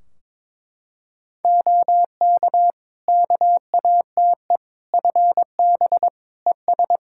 Morse code, heard as a keyed tone, about 22 wpm, 700 Hz. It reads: OK KATE FB ES
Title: Practice QSO - Exchange 2 - Info - 22wpm